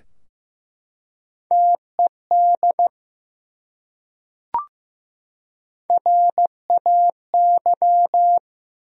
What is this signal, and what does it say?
Morse code, heard as a keyed tone, about 15 wpm, 700 Hz. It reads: TED RAY
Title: USA Common Names - Max 3 Letters - 15wpm